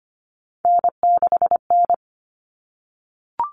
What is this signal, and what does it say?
Morse code, heard as a keyed tone, about 25 wpm, 700 Hz. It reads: N6N
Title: Common Cut Numbers - 3-Digit 25wpm